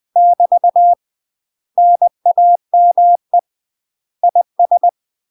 Morse code, heard as a keyed tone, about 20 wpm, 700 Hz. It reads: <BT> NAME IS
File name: UK MASTER QSOs TRAINING / 20WPM / UK MASTER QSOs 0037